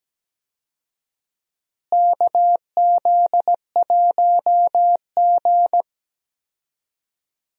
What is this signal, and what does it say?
Morse code, heard as a keyed tone, about 17 wpm, 700 Hz. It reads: KZ1G
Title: USA Call Signs - 4 Characters - Repeated 3x - 17wpm